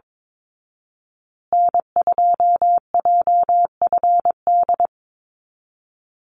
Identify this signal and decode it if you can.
Morse code, heard as a keyed tone, about 22 wpm, 700 Hz. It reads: N2JFD
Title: Call Signs Repeated 3x - 22wpm